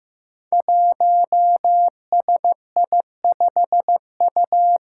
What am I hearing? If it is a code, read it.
Morse code, heard as a keyed tone, about 15 wpm, 700 Hz. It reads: 1SI5U
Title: Five Challenging Letter-Number Combinations 15wpm